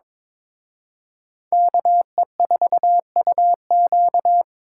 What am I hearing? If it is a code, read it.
Morse code, heard as a keyed tone, about 26 wpm, 700 Hz. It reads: KE4UQ
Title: Speed Racing - USA Call Signs 22wpm